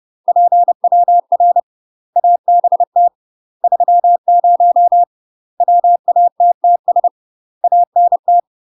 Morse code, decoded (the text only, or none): PWR ABT 30 WATTS ANT